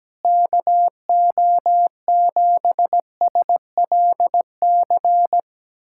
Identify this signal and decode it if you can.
Morse code, heard as a keyed tone, about 17 wpm, 700 Hz. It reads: KO7SLC
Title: USA Callsign DE Callsign - 6 Characters - 17wpm